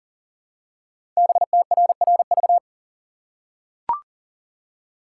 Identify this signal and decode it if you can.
Morse code, heard as a keyed tone, about 39 wpm, 700 Hz. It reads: BTRRV
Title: Code Groups - 5 Letters - 40wpm